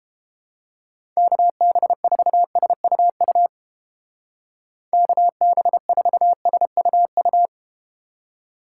Morse code, heard as a keyed tone, about 31 wpm, 700 Hz. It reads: KB4SUU KB4SUU